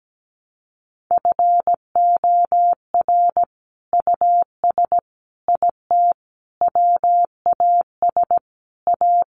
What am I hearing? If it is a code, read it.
Morse code, heard as a keyed tone, about 17 wpm, 700 Hz. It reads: FOR US IT WAS A
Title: Sentences from Top 1200 Words 17wpm